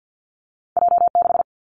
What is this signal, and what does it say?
Morse code, heard as a keyed tone, about 50 wpm, 700 Hz. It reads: J 6
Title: Two Challenging Letter-Number Combinations 50wpm